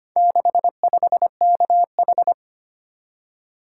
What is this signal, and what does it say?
Morse code, heard as a keyed tone, about 25 wpm, 700 Hz. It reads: 65KH
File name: practice / 25wpm / Four Letter-Number Combinations - 25wpm